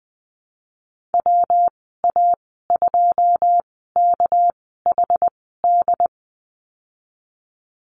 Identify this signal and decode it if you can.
Morse code, heard as a keyed tone, about 20 wpm, spaced as for 14 wpm, 700 Hz. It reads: WA2KHD
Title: 213 Lesson 34g - Number 8 - Call Signs 20wpm